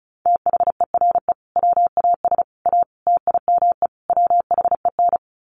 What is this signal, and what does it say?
Morse code, heard as a keyed tone, about 34 wpm, 700 Hz. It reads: THERE WAS A TIME WHEN
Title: Sets of 5 Words 35wpm